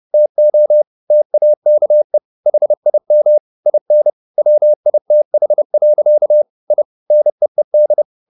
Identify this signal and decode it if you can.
Morse code, heard as a keyed tone, about 30 wpm, 600 Hz. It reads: TO TAKE HIM IN WITH. I NEED